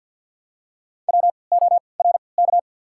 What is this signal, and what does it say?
Morse code, heard as a keyed tone, about 52 wpm, spaced as for 29 wpm, 690 Hz. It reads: WORK
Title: 3x Character Spacing - Top 300 Words - 50wpm